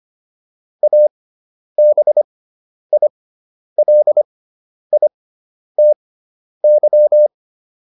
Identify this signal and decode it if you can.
Morse code, heard as a keyed tone, about 25 wpm, spaced as for 10 wpm, 600 Hz. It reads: ABILITY